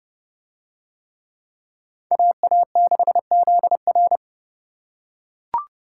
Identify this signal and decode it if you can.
Morse code, heard as a keyed tone, about 29 wpm, 700 Hz. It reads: AA6ZR